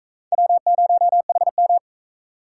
Morse code, heard as a keyed tone, about 40 wpm, 700 Hz. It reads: W0HM